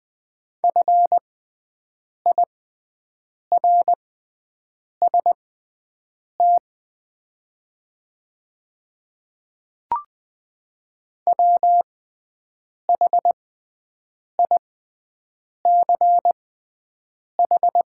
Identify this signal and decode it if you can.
Morse code, heard as a keyed tone, about 20 wpm, spaced as for 7 wpm, 700 Hz. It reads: FIRST WHICH
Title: 6x Character Spacing - Top 100 Words - 20wpm